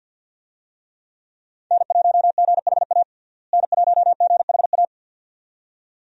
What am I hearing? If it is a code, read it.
Morse code, heard as a keyed tone, about 52 wpm, 700 Hz. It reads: N1GHA N1GHA
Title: USA Call Signs - 5 Characters - Repeated 2x - 50wpm